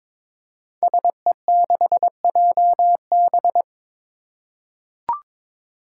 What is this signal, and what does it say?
Morse code, heard as a keyed tone, about 22 wpm, 700 Hz. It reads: SE6JB
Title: Five Challenging Letter-Number Combinations 22wpm